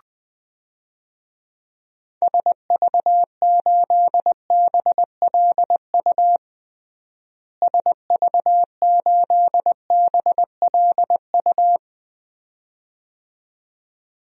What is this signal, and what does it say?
Morse code, heard as a keyed tone, about 21 wpm, 690 Hz. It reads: SV8BLU SV8BLU
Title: DXCC-100 Call Signs - 6 Characters - Repeated 2x - 20wpm